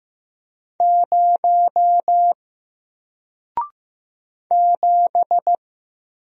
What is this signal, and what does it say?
Morse code, heard as a keyed tone, about 15 wpm, 700 Hz. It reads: 0 7
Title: Single Letter-Number 15wpm